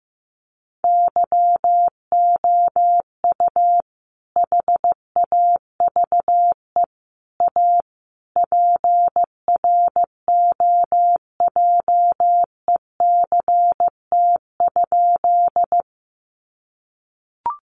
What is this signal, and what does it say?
Morse code, heard as a keyed tone, about 14 wpm, 700 Hz. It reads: YOU HAVE A PROJECT?
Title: Sentences from Top 1400 Words 15wpm